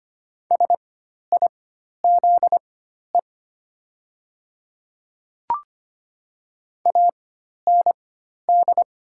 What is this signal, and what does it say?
Morse code, heard as a keyed tone, about 25 wpm, spaced as for 12 wpm, 700 Hz. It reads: SIZE AND
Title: 4x Character Spacing - Top 500 Words - 25wpm